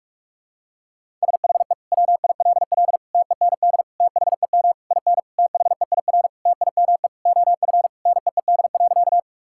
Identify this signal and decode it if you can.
Morse code, heard as a keyed tone, about 41 wpm, 700 Hz. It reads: SHE WILL TEND THEM IN THEIR TIME OF NEED.
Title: Sentences from Top 1000 Words - Review Part 5 - 45wpm